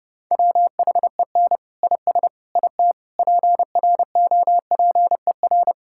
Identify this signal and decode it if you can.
Morse code, heard as a keyed tone, about 30 wpm, 700 Hz. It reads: WHEN IS IT PROPER
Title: Sentences from Top 1000 Words - Review Part 5 - 30wpm